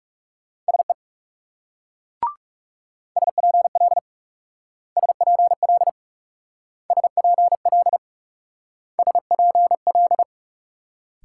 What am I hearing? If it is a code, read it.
Morse code, heard as a keyed tone, about 38 wpm, 700 Hz. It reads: SE SPL SPL SPL SPL